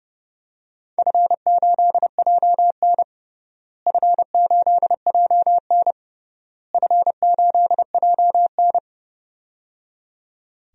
Morse code, heard as a keyed tone, about 30 wpm, 700 Hz. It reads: F8JN F8JN F8JN
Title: DXCC-100 Call Signs - 4 Characters - Repeated 3x - 30wpm